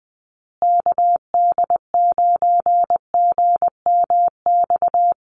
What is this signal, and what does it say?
Morse code, heard as a keyed tone, about 20 wpm, 700 Hz. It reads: KD9GMX
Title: USA Call Signs Repeated 3x 20wpm